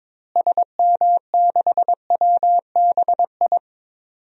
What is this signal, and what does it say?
Morse code, heard as a keyed tone, about 22 wpm, 700 Hz. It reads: SM6WBI
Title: DXCC-100 Call Signs - 6 Characters - Repeated 2x - 22wpm